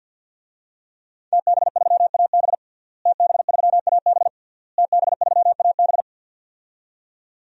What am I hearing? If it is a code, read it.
Morse code, heard as a keyed tone, about 51 wpm, 700 Hz. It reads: TB3AB TB3AB TB3AB
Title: DXCC-100 Call Signs - 5 Characters - Repeated 3x - 50wpm